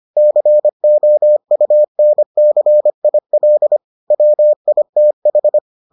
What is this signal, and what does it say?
Morse code, heard as a keyed tone, about 25 wpm, 600 Hz. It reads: COUNCIL WITH